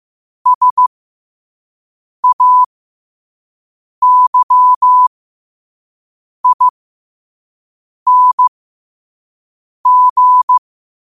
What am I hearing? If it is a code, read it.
Morse code, heard as a keyed tone, about 15 wpm, spaced as for 5 wpm, 1000 Hz. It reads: SAYING
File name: qotdfiles / 5wpm